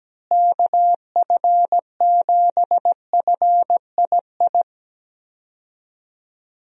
Morse code, heard as a keyed tone, about 17 wpm, 700 Hz. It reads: KF7FII
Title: USA Call Signs - 6 Characters - Repeated 2x - 17wpm